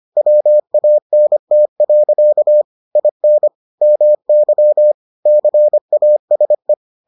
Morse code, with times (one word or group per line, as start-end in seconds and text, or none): 0.2-2.6 WANT.
3.0-3.5 IN
3.8-4.9 MY
5.3-6.7 CASE